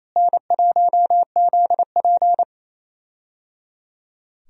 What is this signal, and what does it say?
Morse code, heard as a keyed tone, about 28 wpm, 700 Hz. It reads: N1ZP
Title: USA Call Signs - 4 Characters - Repeated 3x - 28wpm